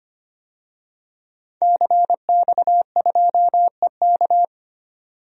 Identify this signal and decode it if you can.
Morse code, heard as a keyed tone, about 25 wpm, 700 Hz. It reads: CX2EK
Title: DXCC-100 Call Signs - 5 Characters - 25wpm